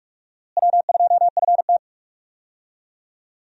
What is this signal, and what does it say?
Morse code, heard as a keyed tone, about 46 wpm, 700 Hz. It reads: W2FT
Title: USA Call Signs - 4 Characters - Repeated 3x - 45wpm